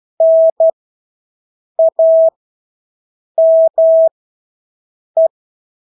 Morse code, heard as a keyed tone, about 12 wpm, spaced as for 6 wpm, 650 Hz. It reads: NAME
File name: Engelske ord 12-6 wpm